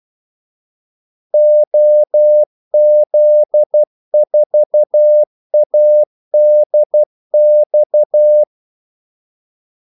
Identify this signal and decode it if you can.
Morse code, heard as a keyed tone, about 12 wpm, 600 Hz. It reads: OZ4ADX
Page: 2 ekstra ordmellemrum mellem kaldesignalerne